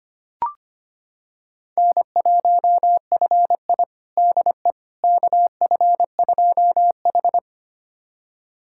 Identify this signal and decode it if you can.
Morse code, heard as a keyed tone, about 25 wpm, 700 Hz. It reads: N1FI DE KF2H
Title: USA Callsign DE Callsign - 4 Characters - 25wpm